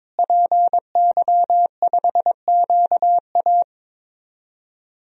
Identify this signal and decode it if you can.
Morse code, heard as a keyed tone, about 22 wpm, 700 Hz. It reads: PY5QA
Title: DXCC-100 Call Signs - 5 Characters - Repeated 3x - 22wpm